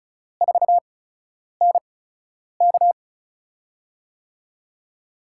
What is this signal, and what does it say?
Morse code, heard as a keyed tone, about 34 wpm, spaced as for 10 wpm, 700 Hz. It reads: 4NK
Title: USA Call Signs - 5 Characters - 8x Character Spacing - 35wpm